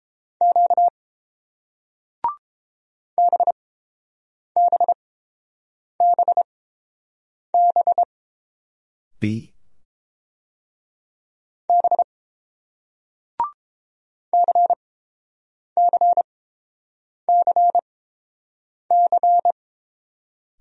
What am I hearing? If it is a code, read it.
Morse code, heard as a keyed tone, about 30 wpm, spaced as for 7 wpm, 690 Hz. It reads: Q BBBB B CCCC